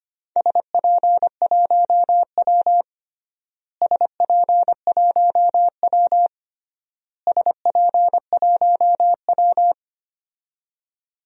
Morse code, heard as a keyed tone, about 25 wpm, 700 Hz. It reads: SP1W SP1W SP1W
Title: DXCC-100 Call Signs - 4 Characters - Repeated 3x - 25wpm